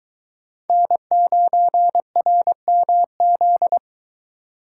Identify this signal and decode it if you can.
Morse code, heard as a keyed tone, about 22 wpm, 700 Hz. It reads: N9RMZ